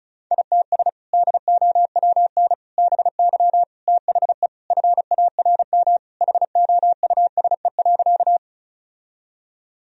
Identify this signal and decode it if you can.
Morse code, heard as a keyed tone, about 36 wpm, 700 Hz. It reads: ITS DOWN BY THE FARM HOUSE.
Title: Sentences from Top 1000 Words - Review Part 2 - 35wpm